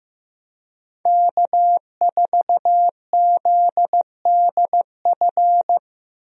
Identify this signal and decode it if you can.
Morse code, heard as a keyed tone, about 15 wpm, 700 Hz. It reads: K4ZDF
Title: USA Call Signs - 5 Characters - Repeated 3x - 15wpm